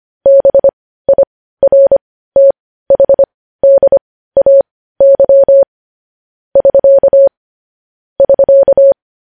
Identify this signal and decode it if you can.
Morse code, heard as a keyed tone, about 25 wpm, spaced as for 15 wpm, 550 Hz. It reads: BIRTHDAY <SK> <SK>
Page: News Headlines in Morse Code at 15 WPM